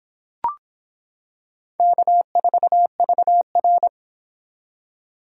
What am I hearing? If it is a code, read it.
Morse code, heard as a keyed tone, about 26 wpm, 700 Hz. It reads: K4VR